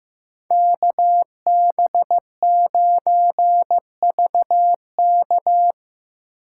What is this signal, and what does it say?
Morse code, heard as a keyed tone, about 16 wpm, 700 Hz. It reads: KB9VK
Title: USA Call Signs - 5 Characters - 2x Character Spacing - 15wpm